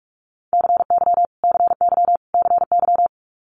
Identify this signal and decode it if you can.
Morse code, heard as a keyed tone, about 46 wpm, 700 Hz. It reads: CY CY CY
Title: Multi-letter Phonemes Repeated 3x 45wpm